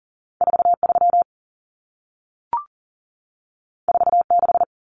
Morse code, heard as a keyed tone, about 40 wpm, 700 Hz. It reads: 43 46
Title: Common Cut Numbers - 2-Digit 40wpm